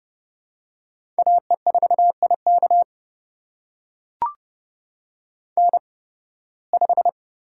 Morse code, heard as a keyed tone, about 30 wpm, spaced as for 20 wpm, 700 Hz. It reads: AE4IK N5
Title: USA Call Signs - 5 Characters - 8x Character Spacing - 30wpm